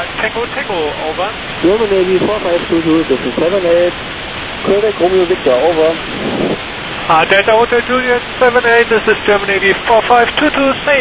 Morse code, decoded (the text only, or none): none